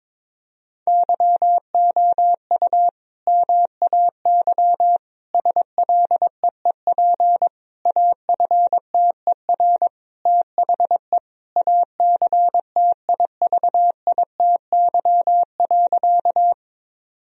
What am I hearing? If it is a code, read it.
Morse code, heard as a keyed tone, about 21 wpm, 700 Hz. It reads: YOU MAY SLEEP AFTER THE ACTIVITY.
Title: Sentences from Top 1000 Words - Review Part 4 - 22wpm